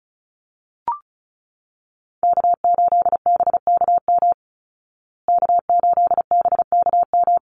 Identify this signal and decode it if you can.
Morse code, heard as a keyed tone, about 34 wpm, 700 Hz. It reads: K8BKM K8BKM
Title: Call Signs Repeated 3x - 35wpm